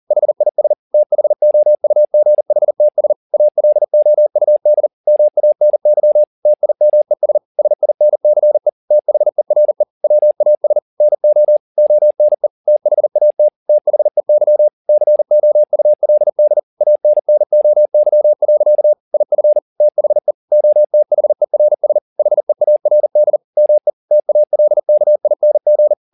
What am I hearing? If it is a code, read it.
Morse code, heard as a keyed tone, about 40 wpm, 600 Hz. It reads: HIS THOUGHTS ALOUD MANY TIMES SINCE THERE WAS NO ONE THAT THEY COULD ANNOY. IF THE OTHERS HEARD ME TALKING